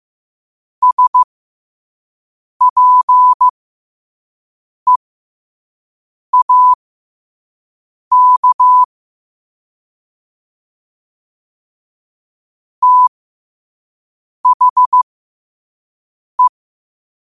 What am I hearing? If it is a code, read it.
Morse code, heard as a keyed tone, about 15 wpm, spaced as for 5 wpm, 1000 Hz. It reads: SPEAK THE
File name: qotdfiles / 5wpm